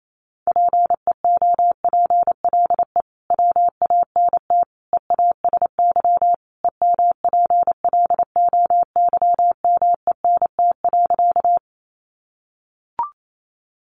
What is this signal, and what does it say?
Morse code, heard as a keyed tone, about 28 wpm, 700 Hz. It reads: PEOPLE WANT EASY EMPLOYMENT.
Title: Sentences from Top 1500 Words 28wpm